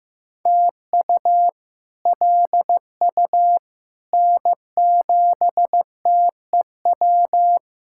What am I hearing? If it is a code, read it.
Morse code, heard as a keyed tone, about 15 wpm, 700 Hz. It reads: TU LU N7TEW
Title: K1USN SST QSO Practice - 15wpm